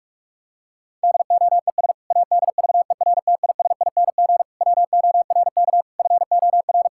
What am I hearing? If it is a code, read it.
Morse code, heard as a keyed tone, about 46 wpm, 700 Hz. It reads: DOES ADVERTISING WORK FOR
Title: Sentences from Top 1000 Words - Review Part 6 - 45wpm